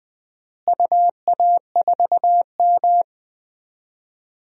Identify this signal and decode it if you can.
Morse code, heard as a keyed tone, about 20 wpm, 700 Hz. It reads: UA4M
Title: DXCC-100 Call Signs - 4 Characters - Repeated 2x - 20wpm